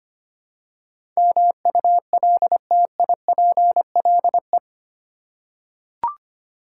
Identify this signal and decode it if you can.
Morse code, heard as a keyed tone, about 25 wpm, 700 Hz. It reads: MULTIPLE
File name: Eight Letter Words - 25wpm